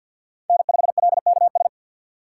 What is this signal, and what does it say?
Morse code, heard as a keyed tone, about 53 wpm, 700 Hz. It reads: N5LKS
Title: USA Call Signs - 6x Character Spacing - 50wpm